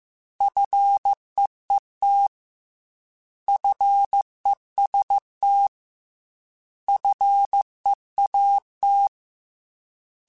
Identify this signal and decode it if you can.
Morse code, heard as a keyed tone, about 15 wpm, 800 Hz. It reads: FEET FEST FEAT
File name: Alphabetical / Alphabetical mp3s / Fs @13